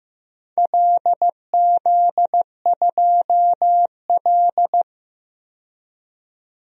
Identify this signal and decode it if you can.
Morse code, heard as a keyed tone, about 15 wpm, 700 Hz. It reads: LZ2L
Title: DXCC-100 Call Signs - 4 Characters - 15wpm